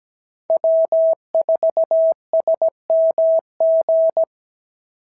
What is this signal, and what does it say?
Morse code, heard as a keyed tone, about 16 wpm, 650 Hz. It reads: W4SMG
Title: SOTA QSO Practice 17wpm